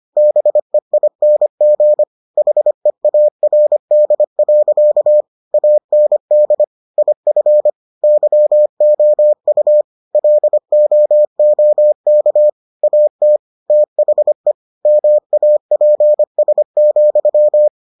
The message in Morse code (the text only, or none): BEING HEARD. AND IF YOU LOOK AT THE MAPS,